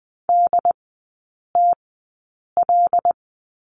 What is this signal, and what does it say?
Morse code, heard as a keyed tone, about 20 wpm, spaced as for 8 wpm, 700 Hz. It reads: DTL
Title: 081 Lesson 14e - Number 2 - Three Characters 20wpm